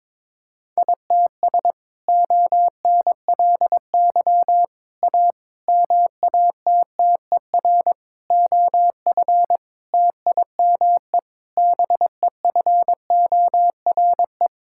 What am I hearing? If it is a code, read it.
Morse code, heard as a keyed tone, about 22 wpm, 700 Hz. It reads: ITS ONLY A MATTER OF TIME BEFORE
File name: practice / 22wpm / Sets of 7 Words - 22wpm